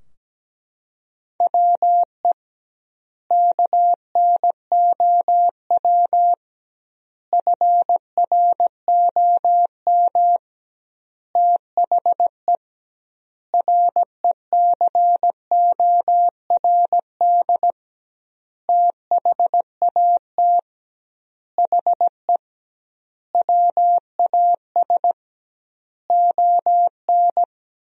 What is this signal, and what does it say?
Morse code, heard as a keyed tone, about 17 wpm, 700 Hz. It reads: WE KNOW FROM THE RECORD THAT HE WAS ON
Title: 2x Word-Spacing - Sentences from Top 500 Words - 17wpm